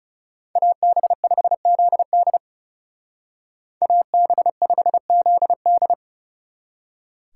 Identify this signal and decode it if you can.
Morse code, heard as a keyed tone, about 38 wpm, 700 Hz. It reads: AB5ZD AB5ZD